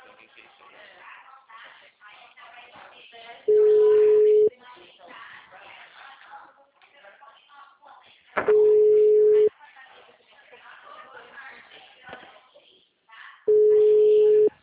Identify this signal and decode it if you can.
Ringback tone (Europe, 425 Hz), with speech.